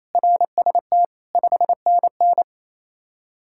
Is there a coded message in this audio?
Yes: Morse code from 0.2 to 2.4 s, heard as a keyed tone, about 36 wpm, 700 Hz. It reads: RST 5NN